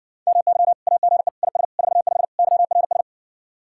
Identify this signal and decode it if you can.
Morse code, heard as a keyed tone, about 69 wpm, 700 Hz. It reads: MY AGE IS 45 YRS